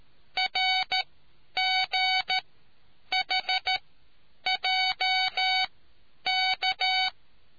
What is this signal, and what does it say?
Morse code, heard as a keyed tone, about 13 wpm, spaced as for 9 wpm, 2250 Hz. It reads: RGHJK